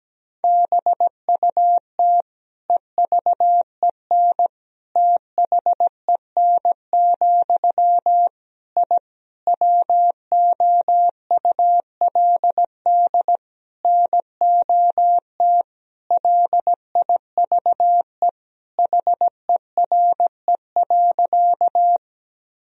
Morse code, heard as a keyed tone, about 17 wpm, 700 Hz. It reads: BUT EVEN THEN, I WOULD NOT LIVE HERE.
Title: Sentences from Top 1000 Words - Review Part 1 - 17wpm